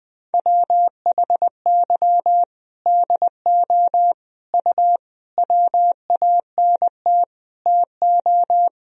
Morse code, heard as a keyed tone, about 20 wpm, 700 Hz. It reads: WHY DO U WANT TO
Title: Sentences from Top 200 Words and Abbreviations 20wpm